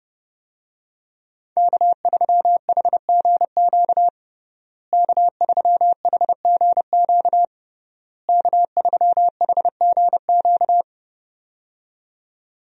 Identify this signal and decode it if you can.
Morse code, heard as a keyed tone, about 32 wpm, 700 Hz. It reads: K3HGQ K3HGQ K3HGQ
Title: USA Call Signs - 5 Characters - Repeated 3x - 30wpm